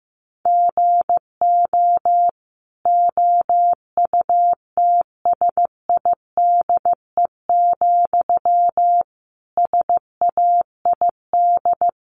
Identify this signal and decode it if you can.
Morse code, heard as a keyed tone, about 15 wpm, 700 Hz. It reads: GO OUTSIDE, SAID
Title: Sentences from Top 900 Words 15wpm